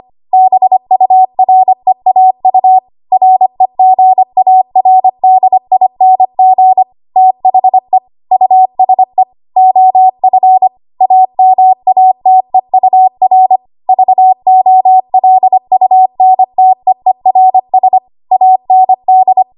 Morse code, heard as a keyed tone, about 25 wpm, 750 Hz. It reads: BUREAU REGARDING THE USE OF AMATEUR VOLUNTEERS AND
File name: Archive / 25 WPM / 25WPM